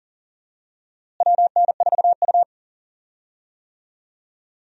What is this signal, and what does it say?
Morse code, heard as a keyed tone, about 40 wpm, 700 Hz. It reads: WN4U
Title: USA Call Signs - 4 Characters - Repeated 2x - 40wpm